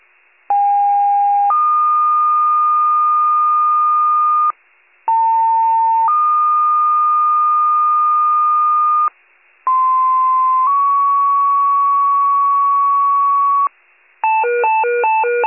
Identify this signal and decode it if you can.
Two-tone page, heard as a one-second tone, then a three-second tone, about 810 then 1220 Hz, about 900 then 1220 Hz, about 1040 then 1120 Hz. It is heard from 0.5 to 13.7 s.
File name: wcalert